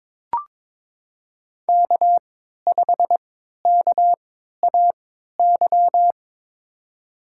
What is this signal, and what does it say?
Morse code, heard as a keyed tone, about 23 wpm, spaced as for 13 wpm, 700 Hz. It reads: K5KAY
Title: USA Call Signs - 5 Characters - 3x Character Spacing - 22wpm